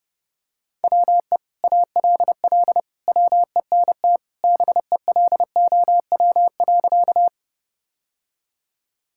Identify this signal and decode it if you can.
Morse code, heard as a keyed tone, about 30 wpm, 700 Hz. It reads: WE ALL WENT BELOW.
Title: Sentences from Top 1000 Words - Review Part 5 - 30wpm